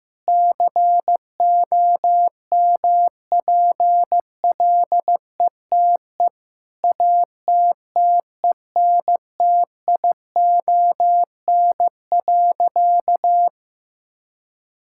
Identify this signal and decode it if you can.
Morse code, heard as a keyed tone, about 15 wpm, 700 Hz. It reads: COMPLETE ATTENTION.
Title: Sentences from Top 1000 Words - Review Part 6 - 15wpm